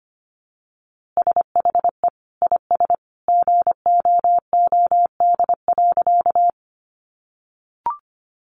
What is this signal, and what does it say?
Morse code, heard as a keyed tone, about 25 wpm, 700 Hz. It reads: SHE IS GOOD.
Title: Easy Phrases 25wpm